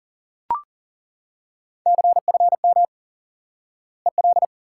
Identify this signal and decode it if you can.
Morse code, heard as a keyed tone, about 40 wpm, 700 Hz. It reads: CFM EL
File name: IQE Course 3 - Lesson 89 - YAGI - Two Elements - 40wpm